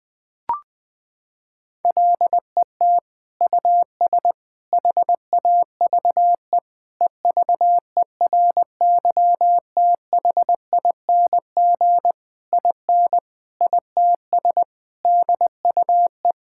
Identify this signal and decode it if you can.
Morse code, heard as a keyed tone, about 20 wpm, 700 Hz. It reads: LET US HAVE EVERYTHING IN ITS DUE
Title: Sentences from Top 1000 Words - Review Part 6 - 20wpm